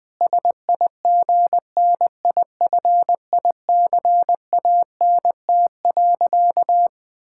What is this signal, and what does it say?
Morse code, heard as a keyed tone, about 20 wpm, 700 Hz. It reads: SIGNIFICANT.
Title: Sentences from Top 1000 Words - Review Part 6 - 20wpm